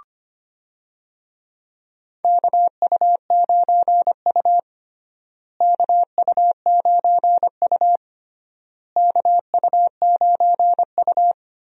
Morse code, heard as a keyed tone, about 25 wpm, 700 Hz. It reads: KU9U KU9U KU9U
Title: USA Call Signs - 4 Characters - Repeated 3x - 25wpm